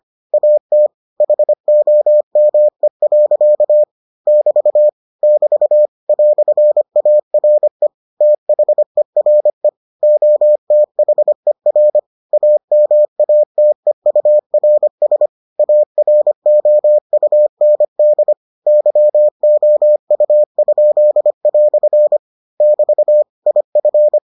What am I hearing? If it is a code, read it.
Morse code, heard as a keyed tone, about 25 wpm, 600 Hz. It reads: AT HOME. <BT> <BT> "ARE THERE OTHER AMATEURS AROUND YOU?" <BT> IF